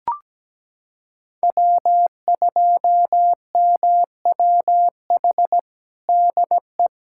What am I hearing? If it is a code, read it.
Morse code, heard as a keyed tone, about 17 wpm, 700 Hz. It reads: W2MWH DE
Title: USA Callsign DE Callsign - 5 Characters - 17wpm